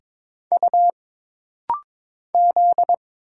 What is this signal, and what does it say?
Morse code, heard as a keyed tone, about 22 wpm, 700 Hz. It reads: U Z
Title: Single Letters - Rapid-Fire 22wpm